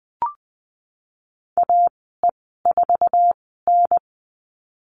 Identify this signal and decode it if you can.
Morse code, heard as a keyed tone, about 20 wpm, spaced as for 14 wpm, 700 Hz. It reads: AE4N
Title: 154 Lesson 25g - Letter p - Call Signs 20wpm